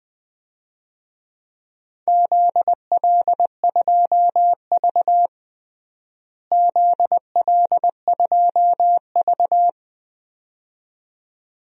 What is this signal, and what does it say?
Morse code, heard as a keyed tone, about 21 wpm, 700 Hz. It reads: ZL2V ZL2V
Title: DXCC-100 Call Signs - 4 Characters - Repeated 2x - 20wpm